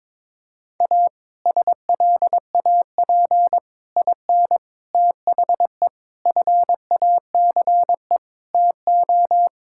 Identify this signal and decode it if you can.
Morse code, heard as a keyed tone, about 22 wpm, 700 Hz. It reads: A SLAP IN THE FACE TO
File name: practice / 22wpm / Sets of 6 Words - 22wpm